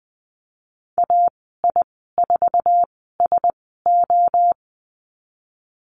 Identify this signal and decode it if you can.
Morse code, heard as a keyed tone, about 20 wpm, spaced as for 15 wpm, 700 Hz. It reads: AI4SO
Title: 055 Lesson 10g - Letter r - Call Signs 20wpm